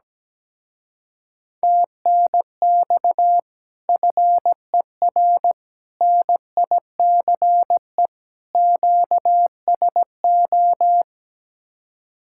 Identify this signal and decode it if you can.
Morse code, heard as a keyed tone, about 17 wpm, 700 Hz. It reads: TNX FER NICE QSO
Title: Speed-Racing Common QSO Phrases Encore - 17wpm